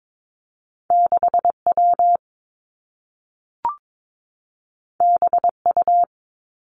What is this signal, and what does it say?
Morse code, heard as a keyed tone, about 22 wpm, 700 Hz. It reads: 6 W B U
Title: Two Challenging Letter-Number Combinations 22wpm